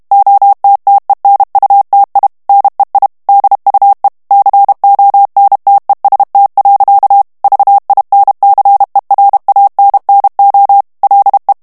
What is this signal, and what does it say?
Morse code, heard as a keyed tone, about 32 wpm, 800 Hz. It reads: OTTENUTI NEI DUE CONTEST. VINCERANNO LE
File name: ITALIEN texte clair 2 vitesse 30 mots minute